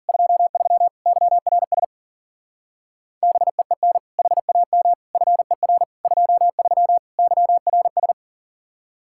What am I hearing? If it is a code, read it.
Morse code, heard as a keyed tone, about 49 wpm, 700 Hz. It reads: 23 YRS BEEN HAM FER 23 YRS
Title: Speed-Racing Common QSO Phrases - 40wpm